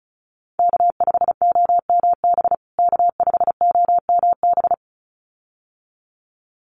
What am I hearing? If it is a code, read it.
Morse code, heard as a keyed tone, about 34 wpm, 700 Hz. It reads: K5OMB K5OMB
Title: USA Call Signs Repeated 2x 35wpm